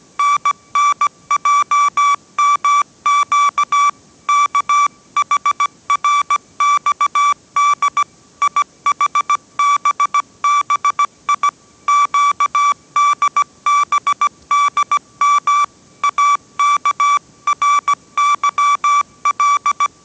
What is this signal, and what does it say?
Morse code, heard as a keyed tone, about 17 wpm, 1160 Hz. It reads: NNJMQ KHRXD IHBBI QDBDM AKRYL
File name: 18wpm